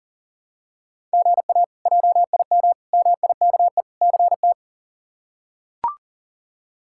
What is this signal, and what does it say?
Morse code, heard as a keyed tone, about 40 wpm, 700 Hz. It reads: GA JIM MIKE CT
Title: K1USN SST QSO Practice - 40wpm